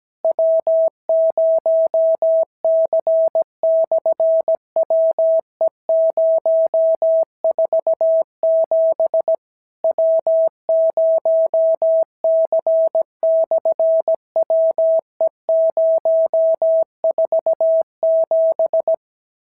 Morse code, heard as a keyed tone, about 17 wpm, 650 Hz. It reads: W0C/WE047 W0C/WE047
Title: SOTA QSO Practice 17wpm